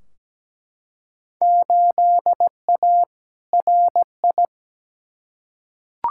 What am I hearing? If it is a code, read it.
Morse code, heard as a keyed tone, about 17 wpm, 700 Hz. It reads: 8A RI
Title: Field Day Exchanges Encore - 17wpm